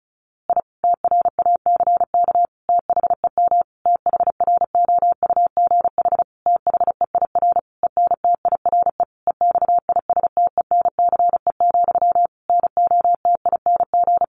Morse code, heard as a keyed tone, about 34 wpm, 700 Hz. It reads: I TRACK THEM THROUGH THEIR ENTIRE EXISTENCE, NOTING
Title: A Sound Of Thunder 35wpm